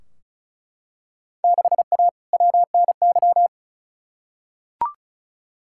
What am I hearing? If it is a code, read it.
Morse code, heard as a keyed tone, about 34 wpm, 700 Hz. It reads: 6A WNY
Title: Field Day Exchanges - 35wpm